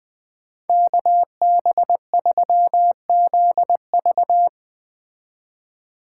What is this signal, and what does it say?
Morse code, heard as a keyed tone, about 20 wpm, 700 Hz. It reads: KB3ZV